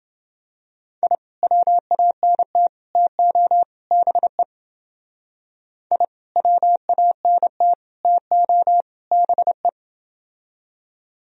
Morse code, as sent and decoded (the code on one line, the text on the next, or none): .. / .-- .- -. - / - --- / -... / .. / .-- .- -. - / - --- / -... .
I WANT TO BE I WANT TO BE